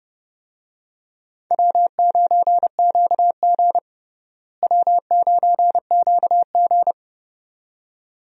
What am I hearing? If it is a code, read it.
Morse code, heard as a keyed tone, about 31 wpm, 700 Hz. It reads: W9QG W9QG
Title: USA Call Signs - 4 Characters - Repeated 2x - 30wpm